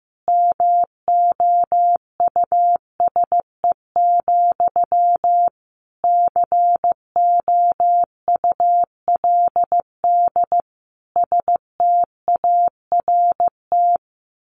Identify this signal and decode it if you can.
Morse code, heard as a keyed tone, about 15 wpm, 700 Hz. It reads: MOUSE, COULD START
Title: A Sound Of Thunder 15wpm